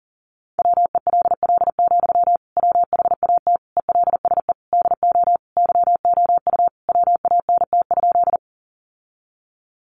Morse code, heard as a keyed tone, about 37 wpm, 700 Hz. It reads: WELL, WHAT ELSE DO YOU WANT?
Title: Sentences from Top 800 Words 40wpm